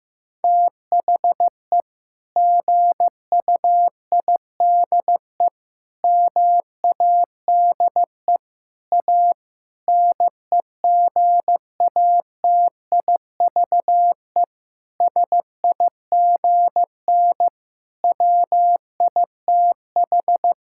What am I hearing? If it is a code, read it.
Morse code, heard as a keyed tone, about 15 wpm, 700 Hz. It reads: THE GUIDE MADE A NEGATIVE SIGN WITH
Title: Sentences from Top 1000 Words - Review Part 6 - 15wpm